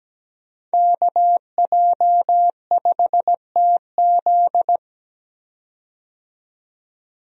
Morse code, heard as a keyed tone, about 17 wpm, 700 Hz. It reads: KJ5TZ
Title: USA Call Signs - 5 Characters - 17wpm